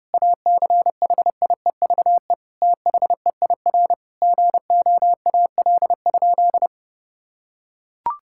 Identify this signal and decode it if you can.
Morse code, heard as a keyed tone, about 30 wpm, 700 Hz. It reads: ACHIEVE THEIR GOAL?
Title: Sentences from Top 1000 Words - Review Part 6 - 30wpm